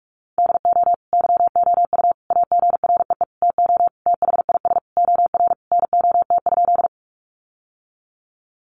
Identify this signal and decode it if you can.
Morse code, heard as a keyed tone, about 44 wpm, 700 Hz. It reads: DO YOU AGREE TO THIS OR NOT?
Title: Sentences from Top 1100 Words 45wpm